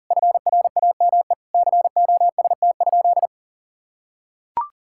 Morse code, heard as a keyed tone, about 40 wpm, 690 Hz. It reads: FRAME COST?
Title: Sentences from Top 1000 Words - Review Part 6 - 40wpm